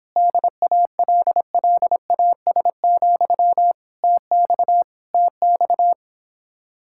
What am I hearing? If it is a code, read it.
Morse code, heard as a keyed tone, about 26 wpm, 700 Hz. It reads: DALLAS, TX TX